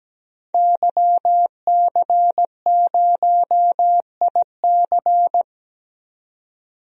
Morse code, heard as a keyed tone, about 17 wpm, 700 Hz. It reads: YC0IC
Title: DXCC-100 Call Signs - 5 Characters - 17wpm